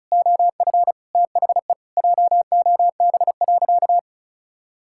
Morse code, heard as a keyed tone, about 34 wpm, 700 Hz. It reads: OF THE JOB.
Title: Sentences from Top 1000 Words - Review Part 5 - 35wpm